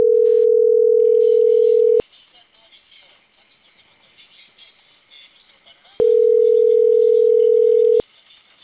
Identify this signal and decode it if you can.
Ringback tone (North America), 440 + 480 Hz.
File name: CENTRAL CAEA / out agcaea02 - .44140